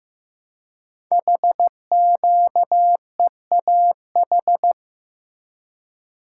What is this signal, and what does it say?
Morse code, heard as a keyed tone, about 15 wpm, 700 Hz. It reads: HQEAH
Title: Code Groups - 5 Letters - 15wpm